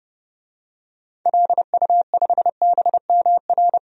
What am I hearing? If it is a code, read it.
Morse code, heard as a keyed tone, about 31 wpm, 700 Hz. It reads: LU5BMR
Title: DXCC-100 Call Signs - 6 Characters - 30wpm